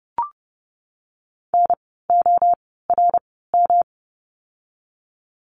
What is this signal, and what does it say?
Morse code, heard as a keyed tone, about 30 wpm, spaced as for 17 wpm, 700 Hz. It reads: NORM
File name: course / 30:10wpm / 108 Lesson 18f - Letter m - Words 30wpm